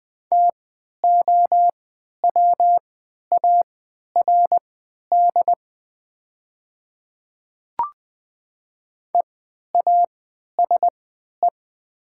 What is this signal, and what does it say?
Morse code, heard as a keyed tone, about 20 wpm, spaced as for 11 wpm, 700 Hz. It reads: TOWARD EASE
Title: 3x Character Spacing - Top 500 Words - 20wpm